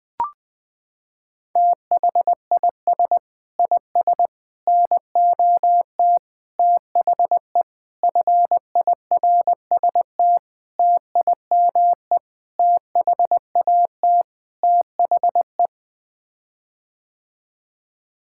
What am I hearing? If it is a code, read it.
Morse code, heard as a keyed tone, about 20 wpm, 700 Hz. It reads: THIS IS NOT THE FIRST TIME THAT THE
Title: Sets of 8 Words - 20wpm